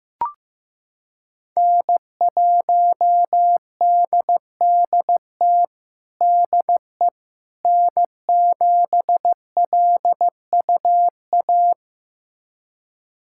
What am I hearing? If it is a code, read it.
Morse code, heard as a keyed tone, about 15 wpm, 700 Hz. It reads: N1DDT DE N7LUA
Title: USA Callsign DE Callsign - 5 Characters - 15wpm